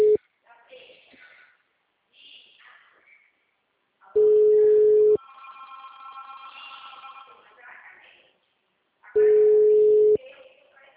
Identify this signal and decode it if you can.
Ringback tone (Europe, 425 Hz), with speech.